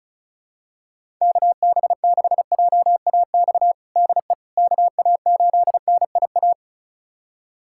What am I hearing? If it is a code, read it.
Morse code, heard as a keyed tone, about 36 wpm, 700 Hz. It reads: KB6JAX DE KA8NIA
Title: USA Callsign DE Callsign - 6 Characters - 35wpm